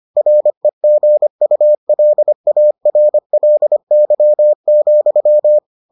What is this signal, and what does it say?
Morse code, heard as a keyed tone, about 25 wpm, 600 Hz. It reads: REGULARLY,